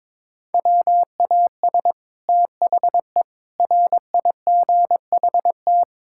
Morse code, heard as a keyed tone, about 22 wpm, 700 Hz. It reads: WAS THE RIGHT
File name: practice / 22wpm / Sets of 6 Words - 22wpm